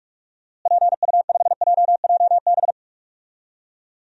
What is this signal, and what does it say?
Morse code, heard as a keyed tone, about 45 wpm, 700 Hz. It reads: PU5JJB
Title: DXCC-100 Call Signs - 6 Characters - Repeated 3x - 45wpm